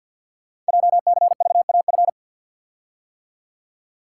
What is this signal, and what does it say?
Morse code, heard as a keyed tone, about 52 wpm, 700 Hz. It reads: JCVAF
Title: Code Groups - 5 Letters - 50wpm